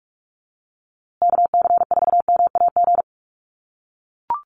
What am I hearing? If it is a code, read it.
Morse code, heard as a keyed tone, about 44 wpm, 700 Hz. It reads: KC4MAG
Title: USA Call Signs Repeated 3x 45wpm